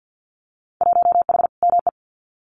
Morse code, heard as a keyed tone, about 50 wpm, 700 Hz. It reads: 1H ME
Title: Winter Field Day Exchanges Encore 50wpm